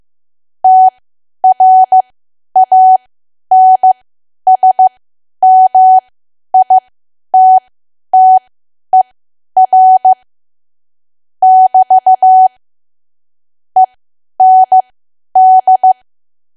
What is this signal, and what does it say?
Morse code, heard as a keyed tone, about 15 wpm, spaced as for 10 wpm, 750 Hz. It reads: TRANSMITTER <BT> END
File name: Archive / 10 WPM / 10WPM